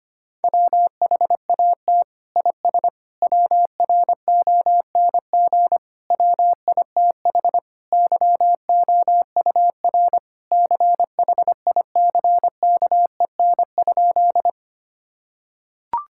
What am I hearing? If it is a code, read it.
Morse code, heard as a keyed tone, about 25 wpm, 700 Hz. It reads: WHAT IS WRONG WITH YOUR CHICKEN?
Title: Sentences from Top 1000 Words - Review Part 5 - 25wpm